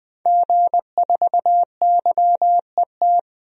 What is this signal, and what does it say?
Morse code, heard as a keyed tone, about 20 wpm, 700 Hz. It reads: G4YET
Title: DXCC-100 Call Signs - 5 Characters - 20wpm